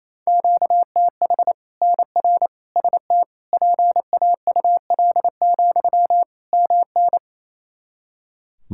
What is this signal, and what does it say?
Morse code, heard as a keyed tone, about 27 wpm, 700 Hz. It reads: QTH NR ST PAUL, MN
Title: K1USN SST QSO Practice 28wpm